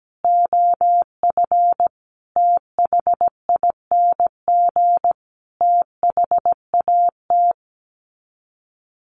Morse code, heard as a keyed tone, about 17 wpm, 700 Hz. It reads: OF THING THAT
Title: Sets of 5 Words Encore 17wpm